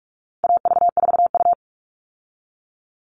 Morse code, heard as a keyed tone, about 44 wpm, 700 Hz. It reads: AV4U
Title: All Cut Numbers - 4-Digit 45wpm